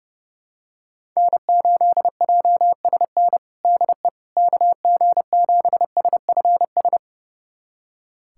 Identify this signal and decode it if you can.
Morse code, heard as a keyed tone, about 32 wpm, 700 Hz. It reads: N8JSN DE KG7SFS
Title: USA Callsign DE Callsign 30wpm